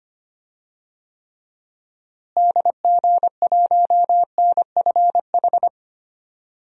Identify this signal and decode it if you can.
Morse code, heard as a keyed tone, about 25 wpm, 700 Hz. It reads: DG1NFH
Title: DXCC-100 Call Signs - 6 Characters - Repeated 3x - 25wpm